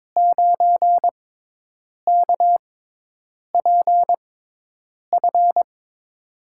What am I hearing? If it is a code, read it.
Morse code, heard as a keyed tone, about 22 wpm, spaced as for 8 wpm, 700 Hz. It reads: 9KPF
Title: USA Call Signs - 6 Characters - 6x Character Spacing - 22wpm